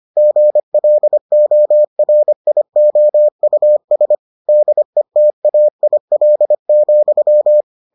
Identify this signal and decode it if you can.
Morse code, heard as a keyed tone, about 25 wpm, 600 Hz. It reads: GLORIOUS DETAIL,